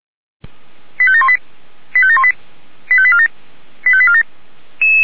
Five-tone selective call (ZVEI), heard as quick runs of five tones, tones of about 70 ms. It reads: 85718 85718 85847 85847